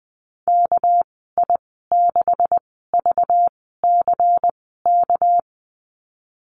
Morse code, heard as a keyed tone, about 19 wpm, 700 Hz. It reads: K I 6 V C K
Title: 200 Lesson 32g - Letter k - Call Signs 20wpm